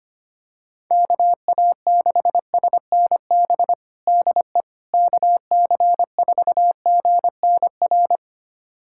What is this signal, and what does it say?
Morse code, heard as a keyed tone, about 25 wpm, 700 Hz. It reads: KA6SNB DE KC4GNR
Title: USA Callsign DE Callsign - 6 Characters - 25wpm